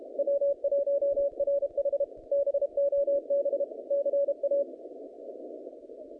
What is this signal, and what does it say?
Morse code, heard as a keyed tone, about 32 wpm, 560 Hz. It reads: W1RH BOB CA
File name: contests / naqp-cw-k5tr-op-k5pi- -0000